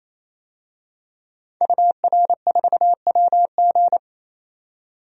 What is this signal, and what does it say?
Morse code, heard as a keyed tone, about 27 wpm, 700 Hz. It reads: UR4WG
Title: DXCC-100 Call Signs - 5 Characters - 28wpm